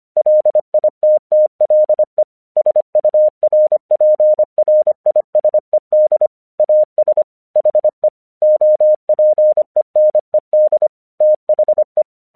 Morse code, heard as a keyed tone, about 26 wpm, 610 Hz. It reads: LITTLE SURPRISED AS HE OPENED THE
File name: 1 English 25wpm / AQC20